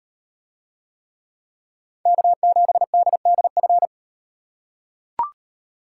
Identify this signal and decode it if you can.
Morse code, heard as a keyed tone, about 38 wpm, 700 Hz. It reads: K7DDF